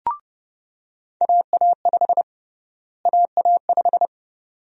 Morse code, heard as a keyed tone, about 30 wpm, 700 Hz. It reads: AA5 AA5
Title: USA Call Sign Prefixes - Repeated 2x - 30wpm